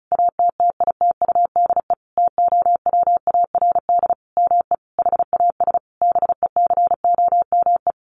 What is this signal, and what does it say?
Morse code, heard as a keyed tone, about 34 wpm, 700 Hz. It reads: ATTITUDE TOWARD ME HAS BECOME
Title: Sentences from Top 1300 Words 35wpm